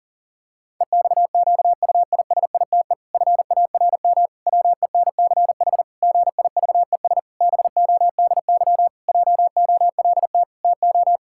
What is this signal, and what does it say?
Morse code, heard as a keyed tone, about 40 wpm, 700 Hz. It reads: EXQUISITE FARM WENCH GIVES BODY JOLT TO
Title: Pangram Challenge v2 - 40wpm